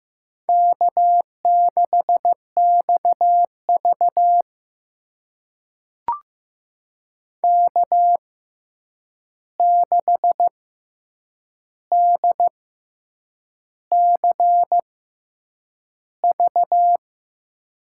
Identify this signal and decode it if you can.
Morse code, heard as a keyed tone, about 15 wpm, spaced as for 7 wpm, 700 Hz. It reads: K6XV K6DCV
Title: USA Call Signs - 6x Character Spacing - 15wpm